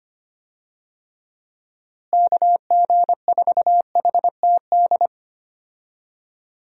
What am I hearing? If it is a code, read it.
Morse code, heard as a keyed tone, about 25 wpm, 700 Hz. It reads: KG4HTD
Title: USA Call Signs - 6 Characters - Repeated 2x - 25wpm